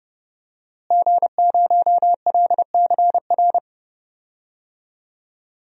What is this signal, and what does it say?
Morse code, heard as a keyed tone, about 30 wpm, 700 Hz. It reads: G0LCR
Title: DXCC-100 Call Signs - 5 Characters - Repeated 2x - 30wpm